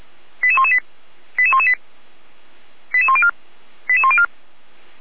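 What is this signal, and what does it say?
Five-tone selective call (ZVEI), heard as quick runs of five tones, tones of about 70 ms. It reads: 88118 88118 88104 88104
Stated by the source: ZVEI five-tone selective call received by SDR.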